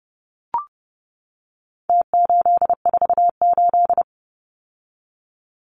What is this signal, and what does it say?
Morse code, heard as a keyed tone, about 30 wpm, 700 Hz. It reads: T848
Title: Numbers - Contest Style - 4-Digit 30wpm